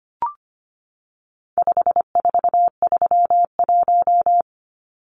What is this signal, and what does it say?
Morse code, heard as a keyed tone, about 25 wpm, 700 Hz. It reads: 5431
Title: Numbers - 4-Digit 25wpm